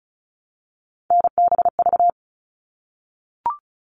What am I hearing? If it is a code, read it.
Morse code, heard as a keyed tone, about 34 wpm, 700 Hz. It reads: NBV
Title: All Cut Numbers - 3-Digit 35wpm